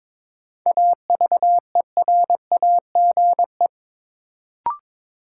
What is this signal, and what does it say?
Morse code, heard as a keyed tone, about 22 wpm, 700 Hz. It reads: AVERAGE
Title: Sets of 5 Words - New Word Drill - 22wpm